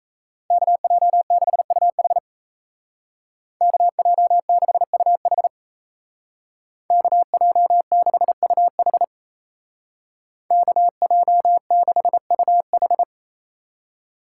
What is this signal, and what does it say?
Morse code, heard as a keyed tone, about 36 wpm, 700 Hz. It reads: KJ6UH KJ6UH KJ6UH KJ6UH